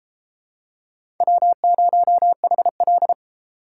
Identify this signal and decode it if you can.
Morse code, heard as a keyed tone, about 30 wpm, 700 Hz. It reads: W0HL